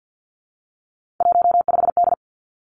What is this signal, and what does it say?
Morse code, heard as a keyed tone, about 50 wpm, 700 Hz. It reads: 15D
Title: Three Challenging Letter-Number Combinations 50wpm